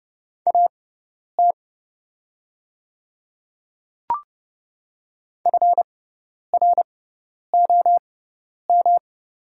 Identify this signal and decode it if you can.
Morse code, heard as a keyed tone, about 30 wpm, spaced as for 10 wpm, 690 Hz. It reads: AT FROM
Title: 6x Character Spacing - Top 100 Words - 30wpm